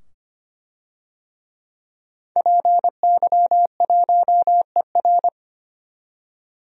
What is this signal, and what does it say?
Morse code, heard as a keyed tone, about 25 wpm, 700 Hz. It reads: PY1ER
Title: DXCC-100 Call Signs - 5 Characters - 25wpm